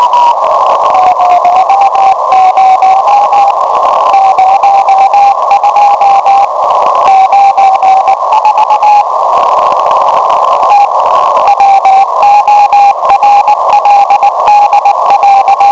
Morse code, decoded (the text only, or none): none